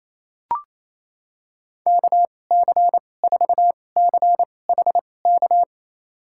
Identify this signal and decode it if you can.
Morse code, heard as a keyed tone, about 28 wpm, spaced as for 20 wpm, 700 Hz. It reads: KC4CHK
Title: USA Call Signs - 2x Character Spacing - 28wpm